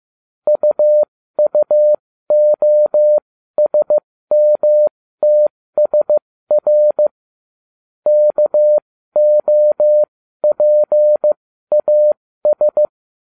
Morse code, heard as a keyed tone, about 15 wpm, spaced as for 13 wpm, 600 Hz. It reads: UUOSMTSR KOPAS